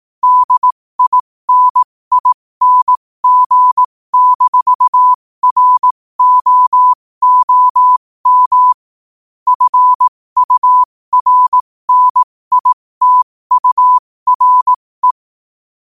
Morse code, heard as a keyed tone, about 18 wpm, spaced as for 15 wpm, 1000 Hz. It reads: DINING-ROOM FURNITURE
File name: qotdfiles / 15wpm